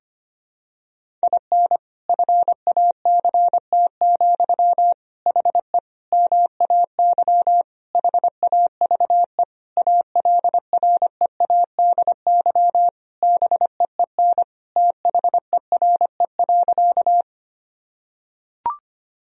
Morse code, heard as a keyed tone, about 25 wpm, 700 Hz. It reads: IN FACT, HE MAY HAVE ALREADY BEEN THERE.
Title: Sentences from Top 1000 Words - Review Part 3 - 25wpm